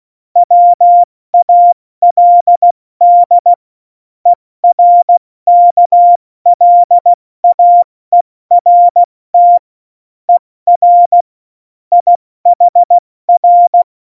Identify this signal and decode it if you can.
Morse code, heard as a keyed tone, about 16 wpm, 700 Hz. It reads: WALD ERKLAERT ER IHR